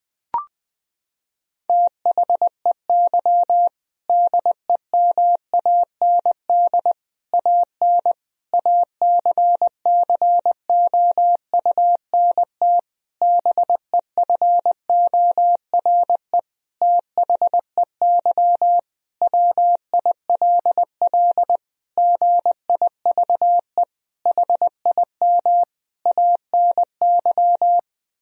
Morse code, heard as a keyed tone, about 20 wpm, 700 Hz. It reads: THEY DEMAND AN ACCOUNT BEFORE THEY WILL GIVE HIM ANY
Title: Sentences from Top 1000 Words - Review Part 4 - 20wpm